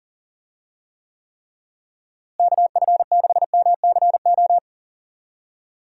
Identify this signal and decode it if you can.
Morse code, heard as a keyed tone, about 40 wpm, 700 Hz. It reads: KF6MCO
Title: USA Call Signs - 6 Characters - Repeated 3x - 40wpm